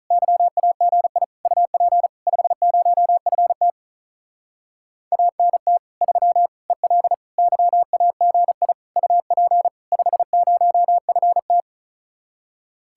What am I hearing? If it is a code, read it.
Morse code, heard as a keyed tone, about 46 wpm, 700 Hz. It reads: YAGI UP 50FT ANT 3 EL YAGI UP 50FT